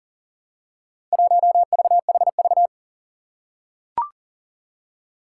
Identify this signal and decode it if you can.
Morse code, heard as a keyed tone, about 40 wpm, 700 Hz. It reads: 1VHV
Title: Four Challenging Letter-Number Combinations 40wpm